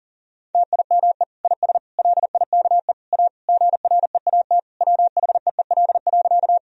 Morse code, heard as a keyed tone, about 40 wpm, 700 Hz. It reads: TIME IS LIKE A GREAT WHEEL.
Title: Sentences from Top 1000 Words - Review Part 3 - 40wpm